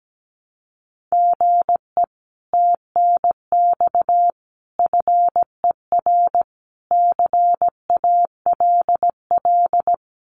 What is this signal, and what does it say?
Morse code, heard as a keyed tone, about 17 wpm, 700 Hz. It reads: GE TNX FER CALL
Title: Common QSO Phrases Encore 17wpm